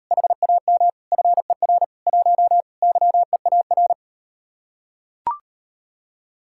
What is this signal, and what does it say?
Morse code, heard as a keyed tone, about 38 wpm, 700 Hz. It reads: HAM FER 1 YEAR